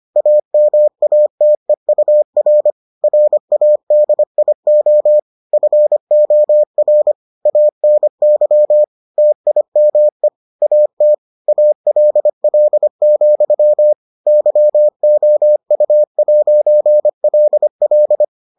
Morse code, heard as a keyed tone, about 25 wpm, 600 Hz. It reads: AMATEUR RADIO FOR ANY TIME AT ALL, YOU'LL